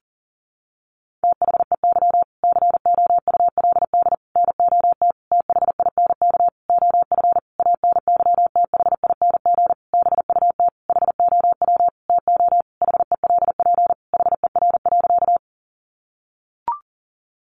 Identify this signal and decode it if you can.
Morse code, heard as a keyed tone, about 40 wpm, 700 Hz. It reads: THEY COULD NOT THINK OF ANYTHING BUT HOW TO HELP HER.
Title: Sentences from Top 1000 Words 40wpm